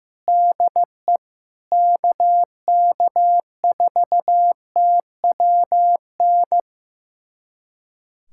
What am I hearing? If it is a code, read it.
Morse code, heard as a keyed tone, about 15 wpm, 700 Hz. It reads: DE KK4TWN
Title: USA Callsign DE Callsign - 6 Characters - 15wpm